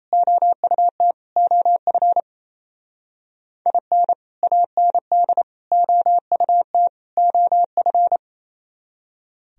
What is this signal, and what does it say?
Morse code, heard as a keyed tone, about 29 wpm, 700 Hz. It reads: OUT OF IN AND OUT OF
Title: Speed-Racing - Sets of 4 Words 28wpm